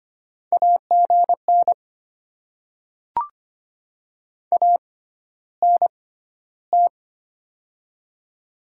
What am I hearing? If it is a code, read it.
Morse code, heard as a keyed tone, about 25 wpm, spaced as for 13 wpm, 700 Hz. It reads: AGN ANT
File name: Top 20 QSO Elements - 6x Character Spacing - 25wpm